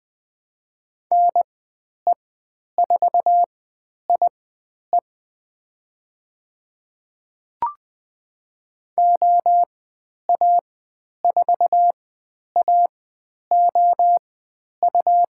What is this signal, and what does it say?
Morse code, heard as a keyed tone, about 20 wpm, spaced as for 10 wpm, 700 Hz. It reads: NE4IE OA4AOU
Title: 109 Lesson 18g - Letter m - Call Signs 20wpm